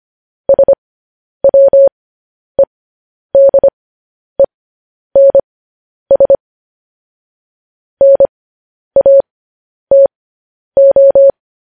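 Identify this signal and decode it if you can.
Morse code, heard as a keyed tone, about 25 wpm, spaced as for 10 wpm, 550 Hz. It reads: SWEDENS NATO
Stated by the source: Morse code practice sending news headlines at 10 wpm.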